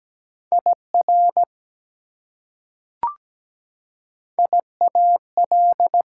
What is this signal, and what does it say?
Morse code, heard as a keyed tone, about 17 wpm, 700 Hz. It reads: IR IAL
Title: Prefixes and Suffixes 17wpm